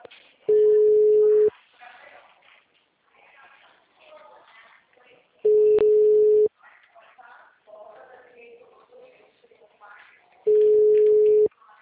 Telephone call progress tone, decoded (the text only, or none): ringback tone